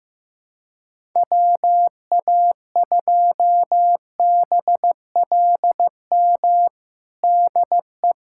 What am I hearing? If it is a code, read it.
Morse code, heard as a keyed tone, about 15 wpm, 700 Hz. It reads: WA2BLM DE
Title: USA Callsign DE Callsign 15wpm